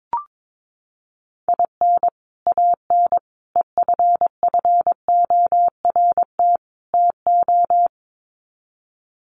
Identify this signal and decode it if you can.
Morse code, heard as a keyed tone, about 22 wpm, 700 Hz. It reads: IN AN EFFORT TO
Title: Sets of 4 Words 22wpm